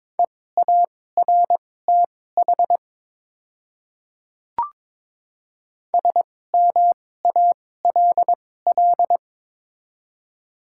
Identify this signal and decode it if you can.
Morse code, heard as a keyed tone, about 22 wpm, spaced as for 16 wpm, 700 Hz. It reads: EARTH SMALL
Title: 2x Character Spacing - Top 200 Words - 22wpm